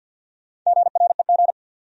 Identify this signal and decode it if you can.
Morse code, heard as a keyed tone, about 50 wpm, 700 Hz. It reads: GREG
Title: USA Common Names - Max 4 Letters - 50wpm